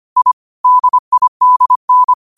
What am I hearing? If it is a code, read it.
Morse code, heard as a keyed tone, about 25 wpm, 1000 Hz. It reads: I DIDN
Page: Quotes for Fri, 15 Aug 2025 in Morse Code at 25 words per minute.